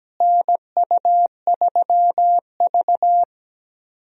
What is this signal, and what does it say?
Morse code, heard as a keyed tone, about 17 wpm, 700 Hz. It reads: NU3V
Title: USA Call Signs - 4 Characters - Repeated 3x - 17wpm